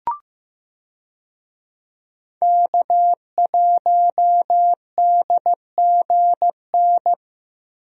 Morse code, heard as a keyed tone, about 15 wpm, 700 Hz. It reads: K1DGN
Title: USA Call Signs - 5 Characters - Repeated 2x - 15wpm